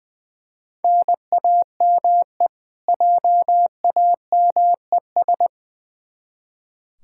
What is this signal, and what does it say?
Morse code, heard as a keyed tone, about 20 wpm, 700 Hz. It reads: NAME JAMES
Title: Practice QSO - Exchange 1 - Beginning - 20wpm